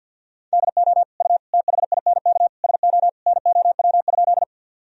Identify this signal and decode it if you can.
Morse code, heard as a keyed tone, about 45 wpm, 700 Hz. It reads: DO U THINK SO NOW?
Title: Sentences from Top 200 Words and Abbreviations 50wpm